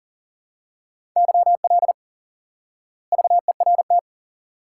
Morse code, heard as a keyed tone, about 40 wpm, 700 Hz. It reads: YL VERT
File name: IQE Course 3 - Lesson 89 - YAGI - Two Elements - 40wpm